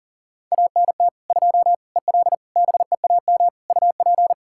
Morse code, heard as a keyed tone, about 40 wpm, 700 Hz. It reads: ANT 2 EL BEAM UP
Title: Practice QSO - Exchange 2 - Info - 40wpm